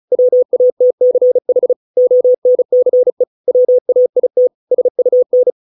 Morse code, heard as a keyed tone, about 34 wpm, 500 Hz. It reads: WATCH ONCE WAIT SUN